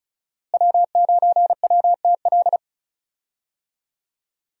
Morse code, heard as a keyed tone, about 36 wpm, 700 Hz. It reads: W9WTL
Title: USA Call Signs - 5 Characters - Repeated 2x - 35wpm